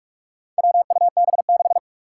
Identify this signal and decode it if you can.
Morse code, heard as a keyed tone, about 41 wpm, 700 Hz. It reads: WUB6
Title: Four Challenging Letter-Number Combinations 45wpm